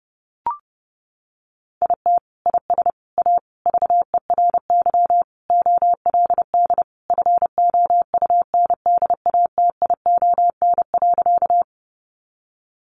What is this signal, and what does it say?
Morse code, heard as a keyed tone, about 30 wpm, 700 Hz. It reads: IT IS A VERY OLD FOUNDATION.
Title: Sentences from Top 1200 Words 30wpm